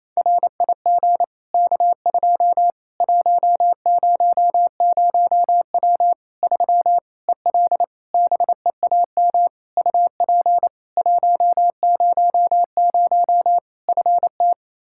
Morse code, heard as a keyed tone, about 27 wpm, 700 Hz. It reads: RIG K2 100W 3 EL BEAM UP 100 FT
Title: K1USN SST QSO Practice 28wpm